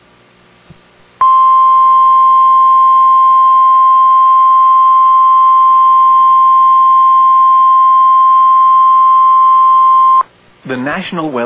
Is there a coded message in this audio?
Yes: Emergency Alert System (SAME) tones (the 1050 Hz attention tone) from 1.1 to 10.3 s, with speech.